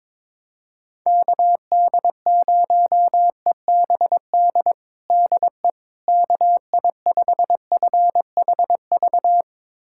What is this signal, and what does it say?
Morse code, heard as a keyed tone, about 22 wpm, 700 Hz. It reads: KD0EBD DE KI5FHV
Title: USA Callsign DE Callsign - 6 Characters - 22wpm